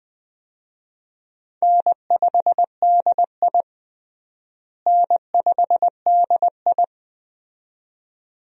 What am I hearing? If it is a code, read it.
Morse code, heard as a keyed tone, about 20 wpm, 700 Hz. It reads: N5DI N5DI
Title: USA Call Signs - 4 Characters - Repeated 2x - 20wpm